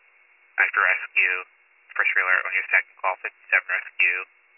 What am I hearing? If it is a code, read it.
Emergency Alert System (SAME) tones, heard as harsh buzzing bursts, with speech. It is heard from 0.6 to 3.9 s.